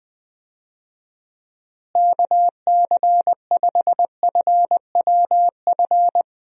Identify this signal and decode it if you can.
Morse code, heard as a keyed tone, about 20 wpm, 700 Hz. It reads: KC5FWF
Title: USA Call Signs - 6 Characters - Repeated 2x - 20wpm